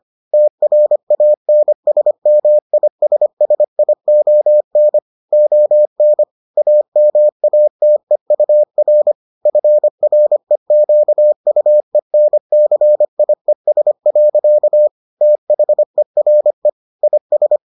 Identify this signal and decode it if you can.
Morse code, heard as a keyed tone, about 25 wpm, 600 Hz. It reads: TRANSMISSION ON AMATEUR FREQUENCIES. THERE IS